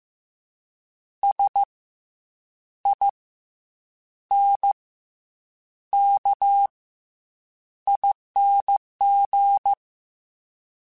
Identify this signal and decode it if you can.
Morse code, heard as a keyed tone, about 15 wpm, 800 Hz. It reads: S I N K ING
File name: Special INGs / Special INGs-mp3s / SlowToFaster ing @13